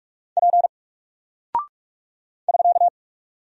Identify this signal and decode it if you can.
Morse code, heard as a keyed tone, about 44 wpm, 700 Hz. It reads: P <SK>
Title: Single Character (Letter:Number:Punctuation:Prosign) Alt 45wpm